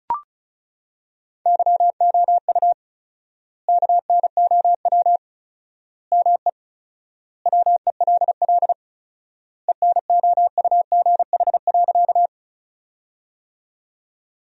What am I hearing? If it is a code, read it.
Morse code, heard as a keyed tone, about 36 wpm, 700 Hz. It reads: YOU KNOW ME WELL ENOUGH.
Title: 4x Word-Spacing - Sentences from Top 400 Words - 35wpm